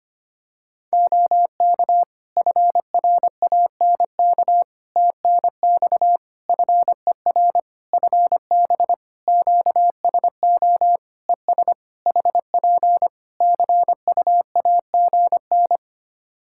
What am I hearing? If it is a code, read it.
Morse code, heard as a keyed tone, about 25 wpm, 700 Hz. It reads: OK FRANK TNX FER FB QSO ES HP CUAGN
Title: Practice QSO - Exchange 4 - Ending - 25wpm